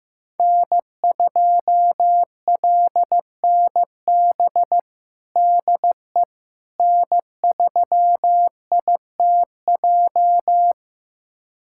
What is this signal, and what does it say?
Morse code, heard as a keyed tone, about 15 wpm, 700 Hz. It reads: N2LNB DE N3ITJ
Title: USA Callsign DE Callsign - 5 Characters - 15wpm